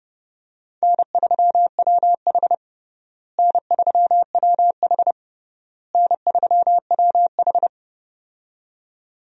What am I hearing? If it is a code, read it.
Morse code, heard as a keyed tone, about 32 wpm, 700 Hz. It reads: N3WH N3WH N3WH
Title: USA Call Signs - 4 Characters - Repeated 3x - 30wpm